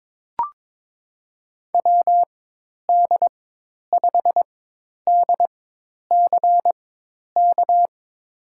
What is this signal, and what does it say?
Morse code, heard as a keyed tone, about 22 wpm, spaced as for 10 wpm, 700 Hz. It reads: WD5DCK
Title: USA Call Signs - 6 Characters - 4x Character Spacing - 22wpm